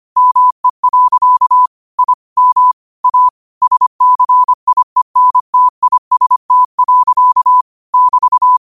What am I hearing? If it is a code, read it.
Morse code, heard as a keyed tone, about 25 wpm, 1000 Hz. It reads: ME. I M A SCIENTIST. <BT>
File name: qotdfiles / 25wpm